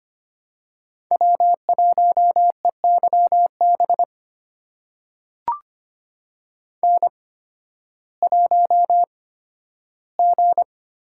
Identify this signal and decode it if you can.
Morse code, heard as a keyed tone, about 25 wpm, spaced as for 13 wpm, 700 Hz. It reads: W1EYB N1G
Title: USA Call Signs - 5 Characters - 8x Character Spacing - 25wpm